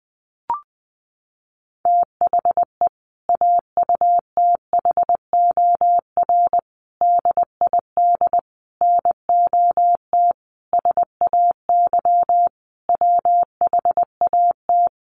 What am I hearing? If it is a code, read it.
Morse code, heard as a keyed tone, about 20 wpm, 700 Hz. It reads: THE AUTHOR DID NOT SAY WHAT
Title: Sentences from Top 1200 Words 20wpm